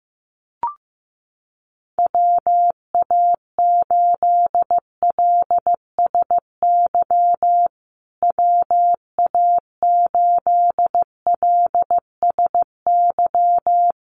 Morse code, heard as a keyed tone, about 15 wpm, 700 Hz. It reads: WA8LSY WA8LSY
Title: USA Call Signs Repeated 3x 15wpm